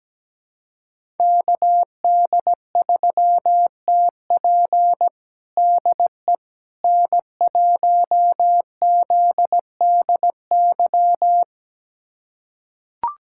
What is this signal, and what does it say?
Morse code, heard as a keyed tone, about 17 wpm, 700 Hz. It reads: KD3TP DE N1ZDY
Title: USA Callsign DE Callsign - 5 Characters - 17wpm